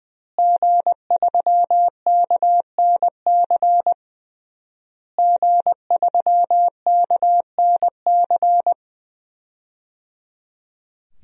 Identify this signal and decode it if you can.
Morse code, heard as a keyed tone, about 20 wpm, 700 Hz. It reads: G3KNC G3KNC
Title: DXCC-100 Call Signs - 5 Characters - Repeated 3x - 20wpm